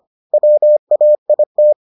Morse code, heard as a keyed tone, about 25 wpm, 600 Hz. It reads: WAIT